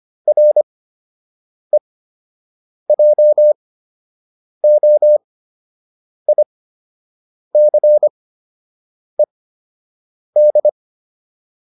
Morse code, heard as a keyed tone, about 25 wpm, spaced as for 7 wpm, 600 Hz. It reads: REJOICED
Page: fundamental-practice-files